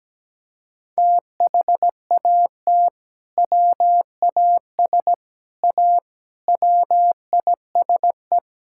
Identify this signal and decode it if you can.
Morse code, heard as a keyed tone, about 17 wpm, 700 Hz. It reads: THAT WAS A WISE
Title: Sentences from Top 1600 Words 17wpm